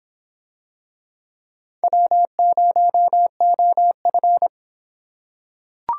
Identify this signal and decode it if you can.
Morse code, heard as a keyed tone, about 23 wpm, 700 Hz. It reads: W0OF